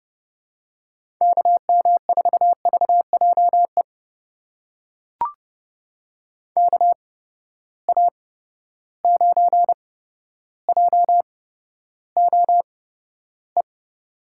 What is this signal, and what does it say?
Morse code, heard as a keyed tone, about 30 wpm, spaced as for 13 wpm, 700 Hz. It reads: KM4VJE KA9JOE
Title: USA Call Signs - 6 Characters - 8x Character Spacing - 30wpm